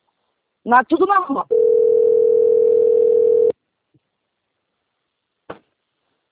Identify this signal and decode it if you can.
Ringback tone (North America), 440 + 480 Hz, with speech.
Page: Braganca - SP